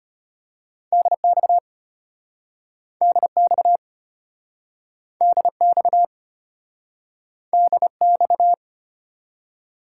Morse code, heard as a keyed tone, about 33 wpm, 700 Hz. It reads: DX DX DX DX